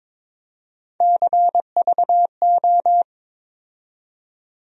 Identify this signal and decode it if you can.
Morse code, heard as a keyed tone, about 21 wpm, 700 Hz. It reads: CVO
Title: USA Call Sign Postfixes - 3 Letters 22wpm